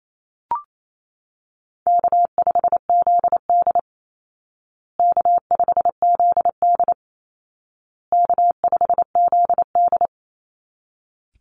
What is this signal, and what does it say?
Morse code, heard as a keyed tone, about 28 wpm, 700 Hz. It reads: K5ZD K5ZD K5ZD
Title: Easy Call Signs Repeated 3x - 28wpm